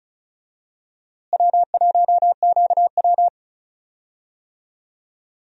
Morse code, heard as a keyed tone, about 34 wpm, 700 Hz. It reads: W1QW
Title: USA Call Signs - 4 Characters - Repeated 2x - 35wpm